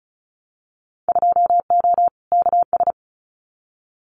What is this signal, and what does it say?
Morse code, heard as a keyed tone, about 34 wpm, 700 Hz. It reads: 2O KS
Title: Winter Field Day Exchanges Encore 35wpm